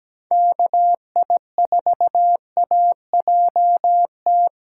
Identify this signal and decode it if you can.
Morse code, heard as a keyed tone, about 20 wpm, 700 Hz. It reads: KI4AJT
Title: Speed Racing - USA Call Signs 17wpm